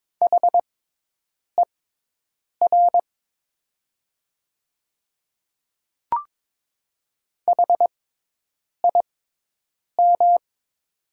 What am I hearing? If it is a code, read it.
Morse code, heard as a keyed tone, about 22 wpm, spaced as for 8 wpm, 700 Hz. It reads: HER HIM
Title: 6x Character Spacing - Top 200 Words - 22wpm